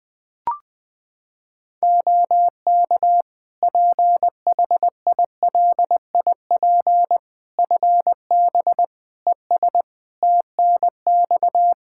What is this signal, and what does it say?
Morse code, heard as a keyed tone, about 21 wpm, 700 Hz. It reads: OK PHILIP FB ES TNX
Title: K1USN SST QSO Practice 20wpm